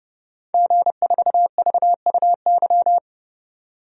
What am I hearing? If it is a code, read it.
Morse code, heard as a keyed tone, about 29 wpm, 700 Hz. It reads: G4VUY
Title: DXCC-100 Call Signs - 5 Characters - 30wpm